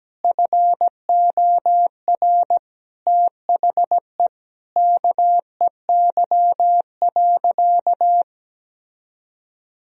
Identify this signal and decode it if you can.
Morse code, heard as a keyed tone, about 17 wpm, 700 Hz. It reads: FOR THE KEY.
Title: Sentences from Top 1000 Words - Review Part 4 - 17wpm